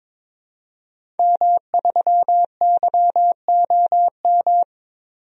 Morse code, heard as a keyed tone, about 22 wpm, 700 Hz. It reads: M3YOM
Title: DXCC-100 Call Signs - 5 Characters - Repeated 3x - 22wpm